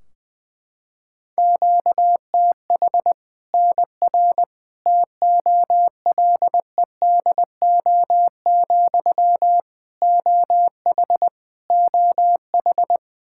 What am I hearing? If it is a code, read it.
Morse code, heard as a keyed tone, about 20 wpm, 700 Hz. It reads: QTH NR TOLEDO, OH OH
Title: Practice QSO - Exchange 1 - Beginning - 20wpm